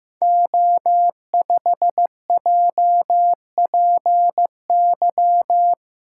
Morse code, heard as a keyed tone, about 15 wpm, 700 Hz. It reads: O5JPY
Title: Code Groups - 5 Letter:Numbers - 15wpm